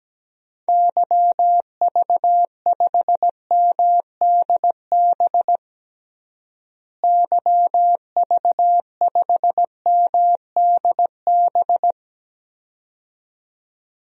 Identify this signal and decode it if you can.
Morse code, heard as a keyed tone, about 17 wpm, 700 Hz. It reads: YV5MDB YV5MDB
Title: DXCC-100 Call Signs - 6 Characters - Repeated 2x - 17wpm